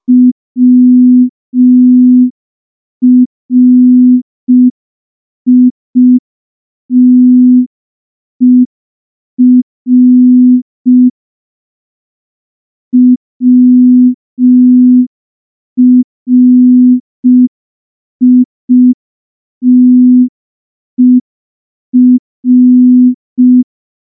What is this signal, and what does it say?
Morse code, heard as a keyed tone, about 5 wpm, 250 Hz. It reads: WRITER WRITER
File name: morse / morse-training-05WPM